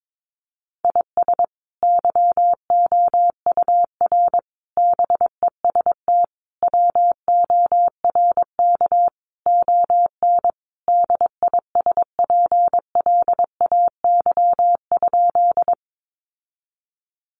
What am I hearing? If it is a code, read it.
Morse code, heard as a keyed tone, about 22 wpm, 700 Hz. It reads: IS YOUR BEST WORK ON DISPLAY?
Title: Sentences from Top 1200 Words 22wpm